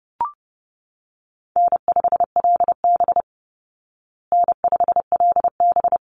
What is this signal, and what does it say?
Morse code, heard as a keyed tone, about 30 wpm, 700 Hz. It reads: N5LB N5LB
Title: Call Signs Repeated 3x - 30wpm